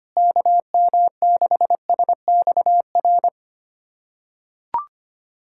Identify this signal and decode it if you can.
Morse code, heard as a keyed tone, about 25 wpm, 700 Hz. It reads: KM6SXR
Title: USA Callsign DE Callsign - 6 Characters - 25wpm